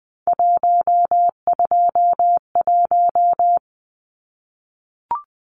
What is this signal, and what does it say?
Morse code, heard as a keyed tone, about 20 wpm, 700 Hz. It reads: 121
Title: Numbers - 3-Digit 20wpm